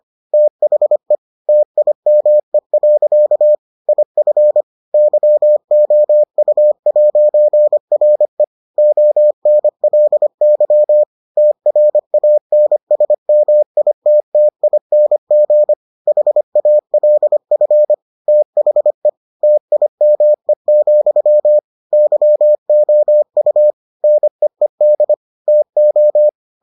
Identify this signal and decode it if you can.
Morse code, heard as a keyed tone, about 25 wpm, 600 Hz. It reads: THE TIME. IF YOU'RE ONLY TRANSMITTING HALF THE TIME, YOU NEED TO